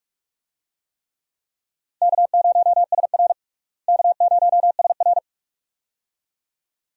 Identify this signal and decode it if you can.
Morse code, heard as a keyed tone, about 42 wpm, 700 Hz. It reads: K0SR K0SR
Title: USA Call Signs - 4 Characters - Repeated 2x - 45wpm